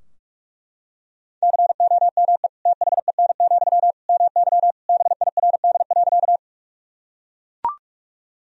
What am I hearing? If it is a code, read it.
Morse code, heard as a keyed tone, about 41 wpm, 700 Hz. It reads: COME THEN, MY BIRD.
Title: Sentences from Top 1000 Words - Review Part 2 - 45wpm